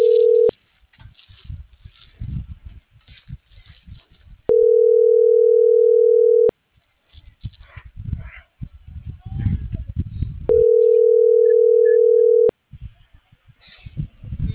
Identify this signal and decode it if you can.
Ringback tone (North America), 440 + 480 Hz.
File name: CENTRAL BLN / out ericka.canales